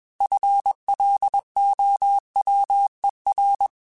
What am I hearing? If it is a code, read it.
Morse code, heard as a keyed tone, about 21 wpm, 800 Hz. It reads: FLOWER
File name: FL Runs / FL Runs mp3s / FL Runs @20